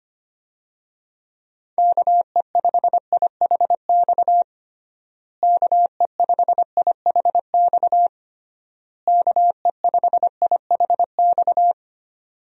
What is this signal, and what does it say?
Morse code, heard as a keyed tone, about 25 wpm, 700 Hz. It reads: KE5IHX KE5IHX KE5IHX
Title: USA Call Signs - 6 Characters - Repeated 3x - 25wpm